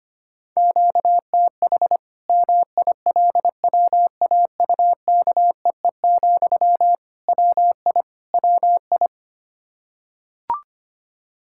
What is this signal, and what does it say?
Morse code, heard as a keyed tone, about 25 wpm, 700 Hz. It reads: QTH MILWAUKEE, WI WI
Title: K1USN SST QSO Practice 25wpm